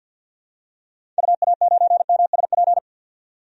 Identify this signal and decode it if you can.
Morse code, heard as a keyed tone, about 51 wpm, 700 Hz. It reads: UA9MSP
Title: DXCC-100 Call Signs - 6 Characters - 50wpm